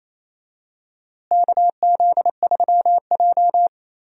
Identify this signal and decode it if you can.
Morse code, heard as a keyed tone, about 28 wpm, 700 Hz. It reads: KZ3J
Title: USA Call Signs - 4 Characters - Repeated 3x - 28wpm